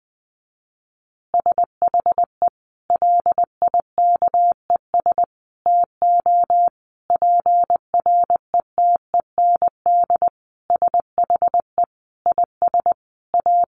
Morse code, heard as a keyed tone, about 20 wpm, 700 Hz. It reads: SHE LIKES TO PRETEND SHE IS A
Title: Additional Easy Phrases 20wpm